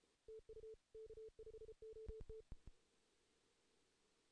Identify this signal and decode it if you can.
Morse code, heard as a keyed tone, about 35 wpm, 440 Hz. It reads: TU K5OT
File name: 2016 JAN CW NAQP / 2016 Jan naqp k5ot - 0-3